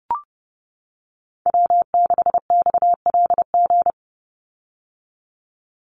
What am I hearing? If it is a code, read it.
Morse code, heard as a keyed tone, about 30 wpm, 700 Hz. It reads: W6XLG
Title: USA Call Signs 30wpm